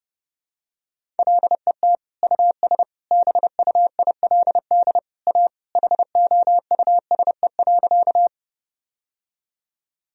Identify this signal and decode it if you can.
Morse code, heard as a keyed tone, about 29 wpm, 700 Hz. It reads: LET US BUILD A HOUSE.
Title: Sentences from Top 1000 Words - Review Part 1 - 30wpm